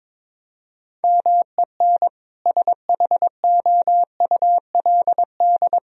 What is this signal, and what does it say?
Morse code, heard as a keyed tone, about 22 wpm, 700 Hz. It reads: MEN SHOULD
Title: Sentences from Top 1000 Words - Review Part 3 - 22wpm